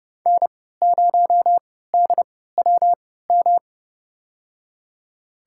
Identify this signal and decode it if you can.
Morse code, heard as a keyed tone, about 30 wpm, spaced as for 17 wpm, 700 Hz. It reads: N0DWM
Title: USA Call Signs - 5 Characters - 3x Character Spacing - 30wpm